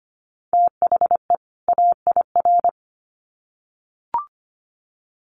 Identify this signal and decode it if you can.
Morse code, heard as a keyed tone, about 25 wpm, 700 Hz. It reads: THE AIR
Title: Sets of 2 Words 25wpm